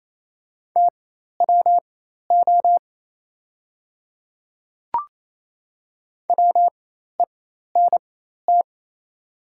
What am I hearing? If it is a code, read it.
Morse code, heard as a keyed tone, about 28 wpm, spaced as for 13 wpm, 700 Hz. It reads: TWO WENT
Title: 4x Character Spacing - Top 300 Words - 28wpm